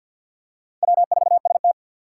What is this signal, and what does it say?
Morse code, heard as a keyed tone, about 51 wpm, 700 Hz. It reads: W4ST
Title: Four Challenging Letter-Number Combinations 50wpm